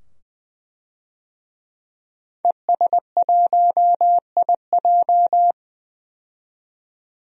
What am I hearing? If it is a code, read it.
Morse code, heard as a keyed tone, about 20 wpm, 700 Hz. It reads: ES1IJ
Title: DXCC-100 Call Signs - 5 Characters - 20wpm